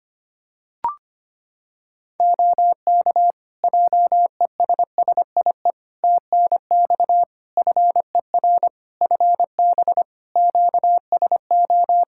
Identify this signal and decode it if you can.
Morse code, heard as a keyed tone, about 25 wpm, 700 Hz. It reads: OK JESSIE TNX FER FB QSO
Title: Practice QSO - Exchange 4 - Ending - 25wpm